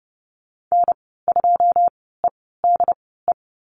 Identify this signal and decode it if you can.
Morse code, heard as a keyed tone, about 30 wpm, spaced as for 17 wpm, 700 Hz. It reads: N2EDE
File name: Lesson 15g - Number 5 - Call Signs 30wpm